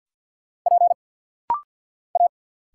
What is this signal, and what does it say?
Morse code, heard as a keyed tone, about 49 wpm, 700 Hz. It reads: P A
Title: Single Letter-Number - Rapid-Fire 50wpm